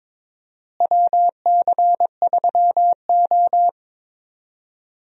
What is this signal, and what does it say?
Morse code, heard as a keyed tone, about 22 wpm, 700 Hz. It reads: WC3O
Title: USA Call Signs - 4 Characters - Repeated 2x - 22wpm